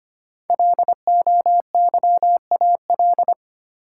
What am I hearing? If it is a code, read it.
Morse code, heard as a keyed tone, about 25 wpm, 700 Hz. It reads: LOYAL
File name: 5 Letter Words - First 3 Letters Given - 25wpm